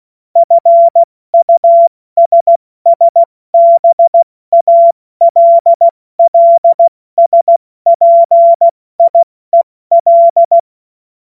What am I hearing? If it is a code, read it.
Morse code, heard as a keyed tone, about 16 wpm, 680 Hz. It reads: FUSSBALLSPIEL
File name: Morsekurs / Witze / Witz 02. Februar 2021, 70 F80